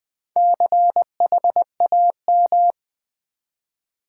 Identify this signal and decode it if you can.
Morse code, heard as a keyed tone, about 21 wpm, 700 Hz. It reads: CHAM
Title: Build A Word - 2 Letters At A Time - 8-Letter Words - 20wpm